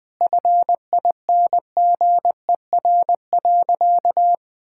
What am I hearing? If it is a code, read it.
Morse code, heard as a keyed tone, about 21 wpm, 700 Hz. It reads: FINGER.
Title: Sentences from Top 1600 Words 20wpm